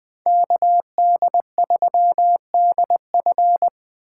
Morse code, heard as a keyed tone, about 20 wpm, 700 Hz. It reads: KD3DF
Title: Speed Racing - USA 2x2 Call Signs 17wpm